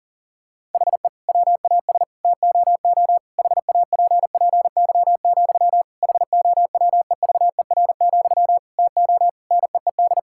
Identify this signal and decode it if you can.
Morse code, heard as a keyed tone, about 40 wpm, 700 Hz. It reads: HE WAS TOO HAPPY, HOWEVER, TO NEED
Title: Sentences from Top 1000 Words - Review Part 6 - 40wpm